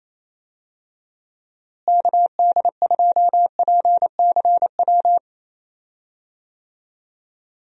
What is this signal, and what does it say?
Morse code, heard as a keyed tone, about 28 wpm, 700 Hz. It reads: KD2PCW
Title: USA Call Signs - 6 Characters - Repeated 2x - 28wpm